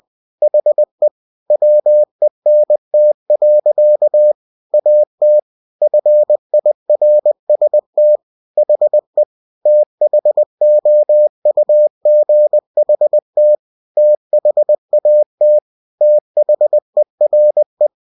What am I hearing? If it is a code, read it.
Morse code, heard as a keyed tone, about 20 wpm, 600 Hz. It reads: HE WENT. AT FIRST HE THOUGHT THAT THERE
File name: cw-books / winnie-the-pooh / winnie-20wpm-page-0041